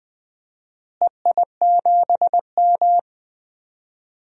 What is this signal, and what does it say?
Morse code, heard as a keyed tone, about 20 wpm, 700 Hz. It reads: EI7M
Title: DXCC-100 Call Signs - 4 Characters - Repeated 2x - 20wpm